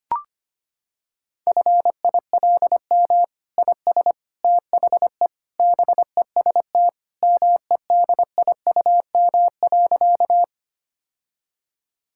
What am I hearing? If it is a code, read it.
Morse code, heard as a keyed tone, about 25 wpm, 700 Hz. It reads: FILM IS THE BEST MEDIUM.
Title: Sentences from Top 1000 Words - Review Part 6 - 25wpm